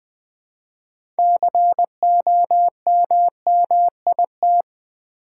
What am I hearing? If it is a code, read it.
Morse code, heard as a keyed tone, about 20 wpm, 700 Hz. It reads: COMMIT
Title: Six Letter Words - 20wpm